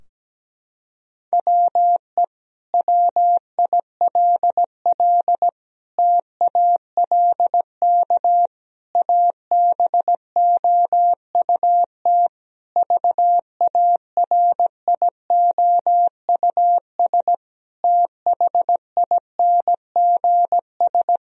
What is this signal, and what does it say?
Morse code, heard as a keyed tone, about 17 wpm, 700 Hz. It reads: WE WILL TALK ABOUT VARIOUS THINGS
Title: Sentences from Top 1000 Words - Review Part 4 - 17wpm